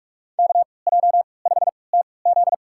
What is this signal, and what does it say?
Morse code, heard as a keyed tone, about 46 wpm, spaced as for 25 wpm, 700 Hz. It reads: KJ5TZ
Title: USA Call Signs - 5 Characters - 3x Character Spacing - 45wpm